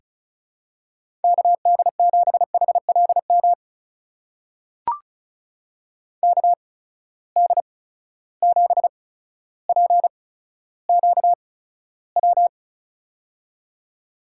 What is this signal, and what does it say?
Morse code, heard as a keyed tone, about 34 wpm, spaced as for 15 wpm, 700 Hz. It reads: KD7HLM KD7PQW
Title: USA Call Signs - 6 Characters - 8x Character Spacing - 35wpm